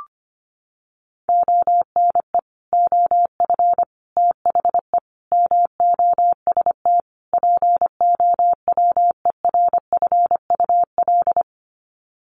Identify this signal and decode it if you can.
Morse code, heard as a keyed tone, about 25 wpm, 700 Hz. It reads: ONE OF THE MOST POWERFUL
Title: Sets of 5 Words Encore 25wpm